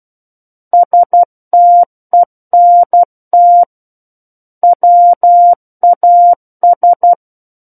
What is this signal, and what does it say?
Morse code, heard as a keyed tone, about 12 wpm, 700 Hz. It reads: STENT WAS
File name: Book One / 03-Chapter 3 - 12 WPM